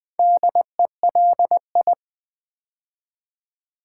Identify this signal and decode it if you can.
Morse code, heard as a keyed tone, about 20 wpm, 700 Hz. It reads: DELI
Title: Build A Word - 4 Letters At A Time - 12-Letter Words - 20wpm